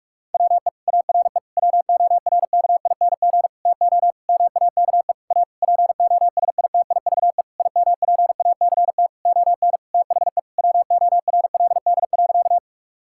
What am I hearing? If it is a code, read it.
Morse code, heard as a keyed tone, about 47 wpm, 700 Hz. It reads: WE ARE WORKING TO MAKE A POSITIVE IMPACT ON THE WORLD.
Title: Sentences from Top 1000 Words - Review Part 6 - 45wpm